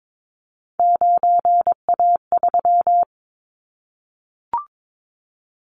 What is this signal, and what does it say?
Morse code, heard as a keyed tone, about 22 wpm, 700 Hz. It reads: 9A3
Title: Callsign Prefixes for Top 50 DXCC - 22wpm